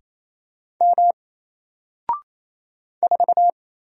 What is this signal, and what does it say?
Morse code, heard as a keyed tone, about 27 wpm, 700 Hz. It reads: M 4
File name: Single-Letter-Number-28wpm